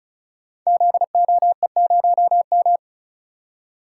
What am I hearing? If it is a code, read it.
Morse code, heard as a keyed tone, about 36 wpm, 690 Hz. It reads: ZOE0M
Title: Code Groups - 5 Letter:Numbers - 35wpm